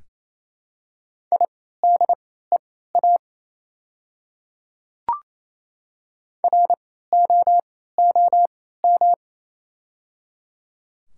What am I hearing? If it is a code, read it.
Morse code, heard as a keyed tone, about 28 wpm, spaced as for 16 wpm, 700 Hz. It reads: IDEA ROOM
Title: 3x Character Spacing - Top 300 Words - 28wpm